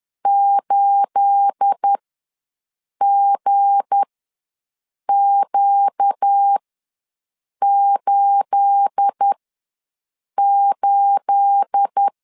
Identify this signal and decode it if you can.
Morse code, heard as a keyed tone, about 10 wpm, spaced as for 6 wpm, 800 Hz. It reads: 8GQ88
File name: LEZIONI / 9 28 ws 10wpm 4124